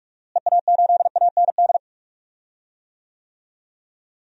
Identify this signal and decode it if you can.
Morse code, heard as a keyed tone, about 46 wpm, 690 Hz. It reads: EA8AND
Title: DXCC-100 Call Signs - 6 Characters - Repeated 2x - 45wpm